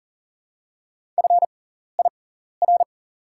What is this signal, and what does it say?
Morse code, heard as a keyed tone, about 40 wpm, spaced as for 14 wpm, 700 Hz. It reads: FIR
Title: 6x Character Spacing - Top 300 Words - 40wpm